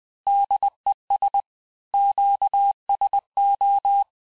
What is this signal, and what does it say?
Morse code, heard as a keyed tone, about 20 wpm, 800 Hz. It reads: DES QSO
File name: cours-du-mercredi / audios / cours-870-FIN-de-QSO-vitesse-19-MPM